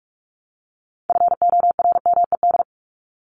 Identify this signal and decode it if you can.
Morse code, heard as a keyed tone, about 45 wpm, 700 Hz. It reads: FORMED
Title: ED Suffix - Top 100 Words with Base Words 45wpm